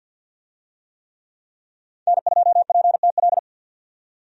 Morse code, heard as a keyed tone, about 50 wpm, 700 Hz. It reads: N2PTL
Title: USA Call Signs - 5 Characters - 50wpm